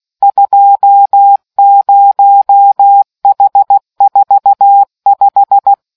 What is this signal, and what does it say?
Morse code, heard as a keyed tone, about 16 wpm, 800 Hz. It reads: 20H45